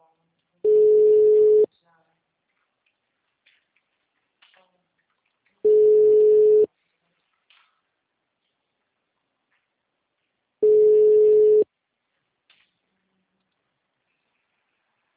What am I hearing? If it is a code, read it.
Ringback tone (Europe, 425 Hz), with speech.
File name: out- -2012- - - .33556